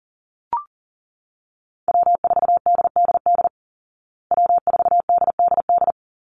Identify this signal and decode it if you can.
Morse code, heard as a keyed tone, about 40 wpm, 700 Hz. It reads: W4DDD W4DDD
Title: Call Signs Repeated 3x - 40wpm